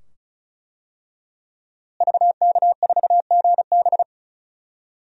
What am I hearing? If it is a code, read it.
Morse code, heard as a keyed tone, about 36 wpm, 700 Hz. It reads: VK4GB
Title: DXCC-100 Call Signs - 5 Characters - Repeated 2x - 35wpm